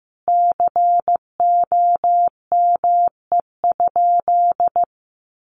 Morse code, heard as a keyed tone, about 14 wpm, 700 Hz. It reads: COME?
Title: Sentences from Top 1000 Words 15wpm